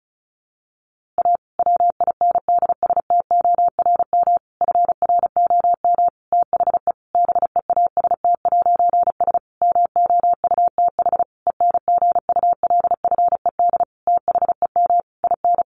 Morse code, heard as a keyed tone, about 34 wpm, 700 Hz. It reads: A WINDSTORM FROM THE BEAST'S MOUTH ENGULFED THEM IN
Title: A Sound Of Thunder 35wpm